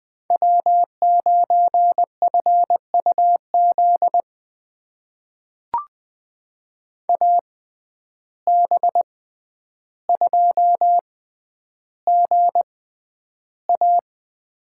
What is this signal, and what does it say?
Morse code, heard as a keyed tone, about 20 wpm, spaced as for 10 wpm, 700 Hz. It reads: W9FUZ AB2GA
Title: USA Call Signs - 5 Characters - 6x Character Spacing - 20wpm